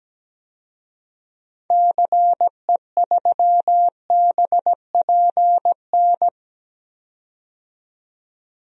Morse code, heard as a keyed tone, about 17 wpm, 700 Hz. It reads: CE3BPN
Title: DXCC-100 Call Signs - 6 Characters - 17wpm